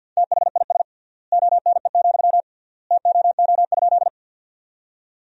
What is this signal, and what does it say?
Morse code, heard as a keyed tone, about 50 wpm, 700 Hz. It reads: THIS ONE, TOO?
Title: 3x Word-Spacing - Sentences from Top 200 Words - 50wpm